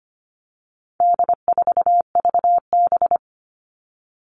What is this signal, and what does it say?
Morse code, heard as a keyed tone, about 25 wpm, 700 Hz. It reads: D4VB
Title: All Cut Numbers - 4-Digit 25wpm